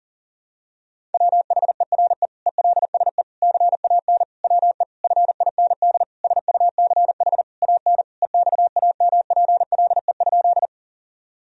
Morse code, heard as a keyed tone, about 40 wpm, 700 Hz. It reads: WHERE ELSE CAN WE FIND SUCH AN EXAMPLE?
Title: Sentences from Top 1000 Words - Review Part 1 - 40wpm